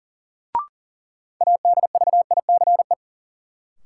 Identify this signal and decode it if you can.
Morse code, heard as a keyed tone, about 40 wpm, 700 Hz. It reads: ADVICE
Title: Sentences from Top 1100 Words - New Word Drill Alt - 40wpm